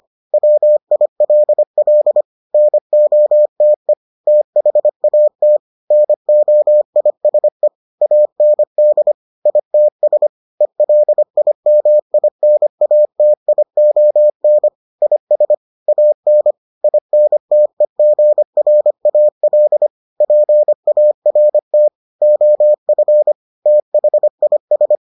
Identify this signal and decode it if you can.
Morse code, heard as a keyed tone, about 25 wpm, 600 Hz. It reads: WILL NOTE THAT NOISE AND ITS ELIMINATION IS AN INTEGRAL PART OF THIS